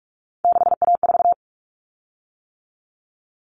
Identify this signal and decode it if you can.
Morse code, heard as a keyed tone, about 44 wpm, 700 Hz. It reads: 6A4
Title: Common Cut Numbers - 3-Digit 45wpm